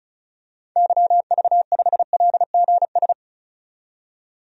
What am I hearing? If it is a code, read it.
Morse code, heard as a keyed tone, about 34 wpm, 700 Hz. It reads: YV5LGS
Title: DXCC-100 Call Signs - 6 Characters - 35wpm